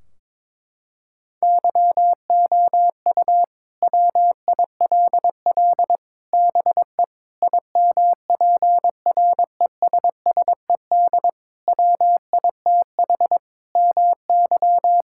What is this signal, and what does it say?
Morse code, heard as a keyed tone, about 22 wpm, 700 Hz. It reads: YOU WILL BE IMPRESSED WITH MY
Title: Sentences from Top 1600 Words 22wpm